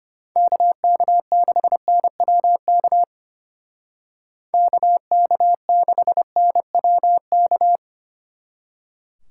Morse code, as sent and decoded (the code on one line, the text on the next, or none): -.- -.- -.... -. .-- -.- / -.- -.- -.... -. .-- -.-
KK6NWK KK6NWK